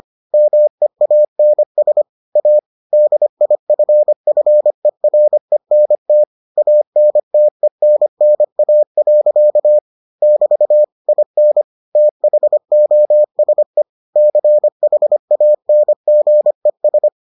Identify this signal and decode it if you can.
Morse code, heard as a keyed tone, about 25 wpm, 600 Hz. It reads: MEANS A DIFFERENT ANTENNA. <BT> IN THOSE CHANGES